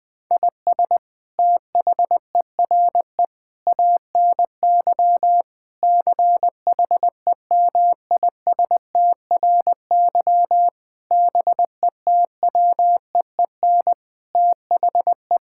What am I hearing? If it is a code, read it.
Morse code, heard as a keyed tone, about 21 wpm, 700 Hz. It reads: IS THERE ANY CHEMISTRY BETWEEN THE
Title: Sentences from Top 1000 Words - Review Part 6 - 20wpm